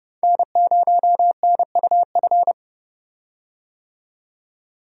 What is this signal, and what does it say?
Morse code, heard as a keyed tone, about 31 wpm, 700 Hz. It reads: N0NUF
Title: USA Call Signs - 5 Characters - 30wpm